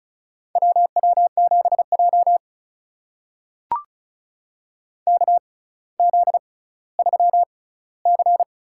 Morse code, heard as a keyed tone, about 36 wpm, spaced as for 18 wpm, 700 Hz. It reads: WW7J KZ3C
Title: USA Call Signs - 4 Characters - 6x Character Spacing - 35wpm